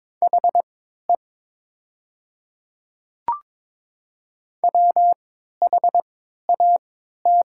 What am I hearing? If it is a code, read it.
Morse code, heard as a keyed tone, about 22 wpm, spaced as for 12 wpm, 700 Hz. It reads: HE WHAT
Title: 3x Character Spacing - Top 200 Words - 22wpm